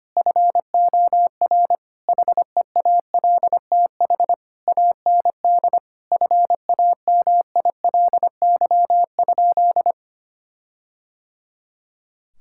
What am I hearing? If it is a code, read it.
Morse code, heard as a keyed tone, about 25 wpm, 700 Hz. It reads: FOR HEALTH AND FAMILY?
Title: Sentences from Top 1600 Words 25wpm